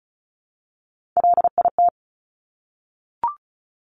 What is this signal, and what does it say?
Morse code, heard as a keyed tone, about 36 wpm, 700 Hz. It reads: LIT
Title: Top Three Letter Combinations in Words 35wpm